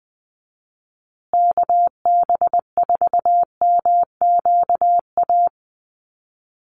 Morse code, heard as a keyed tone, about 20 wpm, 700 Hz. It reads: KB4MQA
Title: USA Call Signs Repeated 3x 20wpm